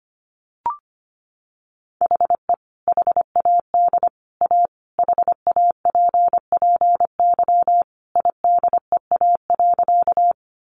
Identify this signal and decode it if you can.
Morse code, heard as a keyed tone, about 25 wpm, 700 Hz. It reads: HE HAD A HAPPY IDEA.
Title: Sentences from Top 1000 Words 25wpm